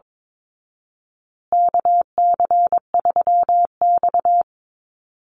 Morse code, heard as a keyed tone, about 22 wpm, 690 Hz. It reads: KC3X
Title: Call Signs Repeated 3x - 22wpm